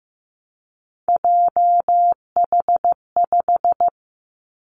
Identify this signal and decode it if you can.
Morse code, heard as a keyed tone, about 15 wpm, 700 Hz. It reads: JH5
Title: Three Challenging Letter-Number Combinations 15wpm